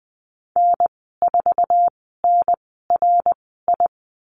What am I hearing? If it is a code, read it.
Morse code, heard as a keyed tone, about 20 wpm, spaced as for 15 wpm, 700 Hz. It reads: N4NRI
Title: 069 Lesson 12g - Letter d - Call Signs 20wpm